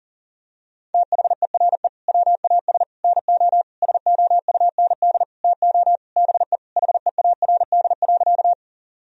Morse code, heard as a keyed tone, about 40 wpm, 700 Hz. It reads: THERE WAS NO SOUND TO BE HEARD.
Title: Sentences from Top 1000 Words - Review Part 1 - 40wpm